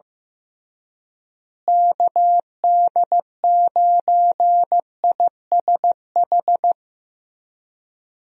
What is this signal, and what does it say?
Morse code, heard as a keyed tone, about 15 wpm, 700 Hz. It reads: KD9ISH
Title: USA Call Signs - 6 Characters - Repeated 3x - 15wpm